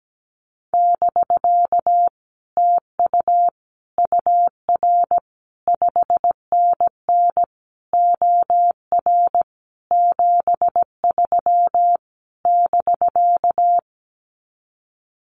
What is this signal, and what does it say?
Morse code, heard as a keyed tone, about 17 wpm, 700 Hz. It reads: <BK> TU UR 5NN OR 73 <BK>
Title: Common QSO Phrases Encore 17wpm